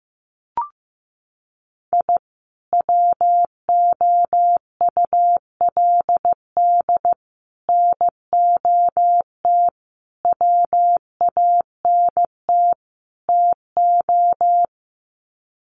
Morse code, heard as a keyed tone, about 15 wpm, 700 Hz. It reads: I WOULD NOT WANT TO
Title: Sets of 5 Words 15wpm